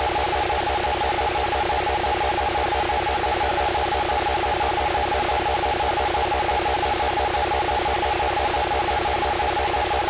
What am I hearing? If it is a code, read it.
Two-tone teleprinter signal, about 450 Hz shift.
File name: Indian Lake EMS 2024 07 03 05 52 50